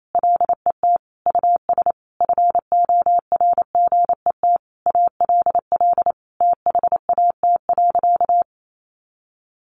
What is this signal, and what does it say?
Morse code, heard as a keyed tone, about 28 wpm, 700 Hz. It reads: LET US FORGET ALL THAT.
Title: Sentences from Top 1200 Words 28wpm